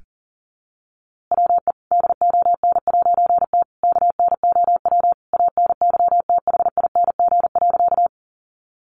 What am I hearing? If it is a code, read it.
Morse code, heard as a keyed tone, about 40 wpm, 700 Hz. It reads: WE DON'T KNOW ANYTHING.
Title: A Sound Of Thunder 40wpm